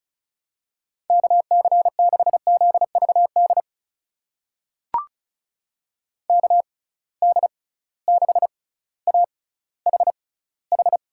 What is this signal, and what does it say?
Morse code, heard as a keyed tone, about 36 wpm, spaced as for 18 wpm, 700 Hz. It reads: KC6ZVD KD6AHH
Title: USA Call Signs - 6 Characters - 6x Character Spacing - 35wpm